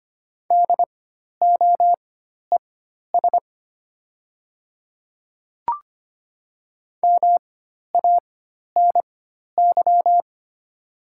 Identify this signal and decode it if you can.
Morse code, heard as a keyed tone, about 25 wpm, spaced as for 12 wpm, 700 Hz. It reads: DOES MANY
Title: 4x Character Spacing - Top 200 Words - 25wpm